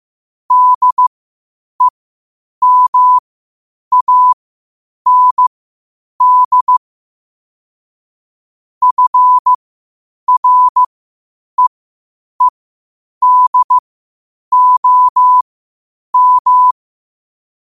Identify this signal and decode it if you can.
Morse code, heard as a keyed tone, about 15 wpm, spaced as for 8 wpm, 1000 Hz. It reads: DEMAND FREEDOM
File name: qotdfiles / 8wpm